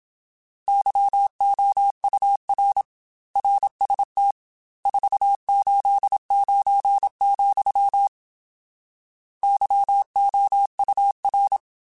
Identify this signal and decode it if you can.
Morse code, heard as a keyed tone, about 27 wpm, 800 Hz. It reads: YOUR RST 489, YOUR
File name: RSTs / RST-mp3s / Your RSTs @25